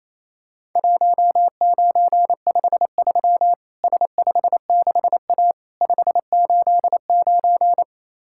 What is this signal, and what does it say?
Morse code, heard as a keyed tone, about 28 wpm, 700 Hz. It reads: 1953 S56A 589
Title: Worked All Europe DX Contest - QTC Exchange - 28wpm